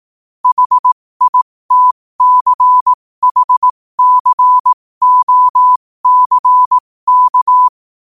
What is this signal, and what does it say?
Morse code, heard as a keyed tone, about 18 wpm, spaced as for 15 wpm, 1000 Hz. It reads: HITCHCOCK
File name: qotdfiles / 15wpm